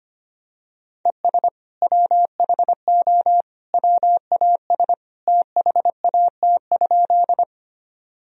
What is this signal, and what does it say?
Morse code, heard as a keyed tone, about 25 wpm, 700 Hz. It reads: ES WHO WAS THAT?
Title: Sentences from Top 100 Words and Abbreviations 25wpm